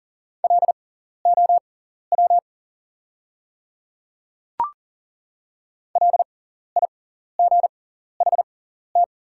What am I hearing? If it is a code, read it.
Morse code, heard as a keyed tone, about 40 wpm, spaced as for 14 wpm, 700 Hz. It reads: LOW LIGHT
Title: 6x Character Spacing - Top 200 Words - 40wpm